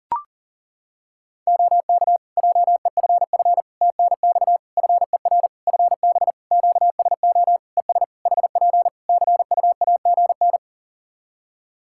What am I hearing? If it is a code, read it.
Morse code, heard as a keyed tone, about 40 wpm, 700 Hz. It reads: OK JEFF TNX FER FB QSO ES HP CUAGN
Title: K1USN SST QSO Practice 40wpm